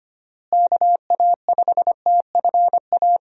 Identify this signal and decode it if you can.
Morse code, heard as a keyed tone, about 25 wpm, 700 Hz. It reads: KA5TFA
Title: USA Call Signs - 6 Characters - Repeated 2x - 25wpm